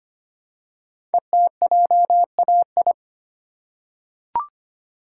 Morse code, heard as a keyed tone, about 25 wpm, 700 Hz. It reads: ETJAI
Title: Code Groups - 5 Letters - 25wpm